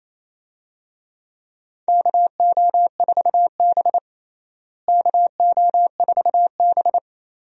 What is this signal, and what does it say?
Morse code, heard as a keyed tone, about 28 wpm, 700 Hz. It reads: KO4B KO4B
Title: USA Call Signs - 4 Characters - Repeated 3x - 28wpm